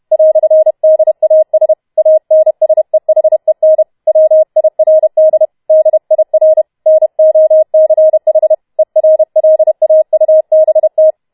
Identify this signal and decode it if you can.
Morse code, heard as a keyed tone, about 31 wpm, 630 Hz. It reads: "DAS ANSEHEN WIRD DIR NOCH ERLAUBT
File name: Schneewittchen Teil4 32wpm